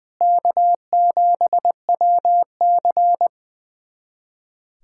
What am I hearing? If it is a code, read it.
Morse code, heard as a keyed tone, about 20 wpm, 700 Hz. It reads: K7WC
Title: USA Callsign DE Callsign - 4 Characters - 20wpm